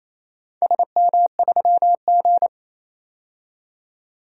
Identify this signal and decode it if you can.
Morse code, heard as a keyed tone, about 28 wpm, 700 Hz. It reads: SM3G
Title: DXCC-100 Call Signs - 4 Characters - 28wpm